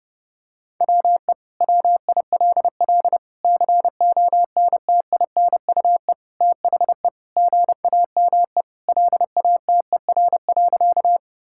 Morse code, heard as a keyed tone, about 32 wpm, 700 Hz. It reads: WE WILL CONTINUE THE GAME LATER.
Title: Sentences from Top 1000 Words - Review Part 6 - 30wpm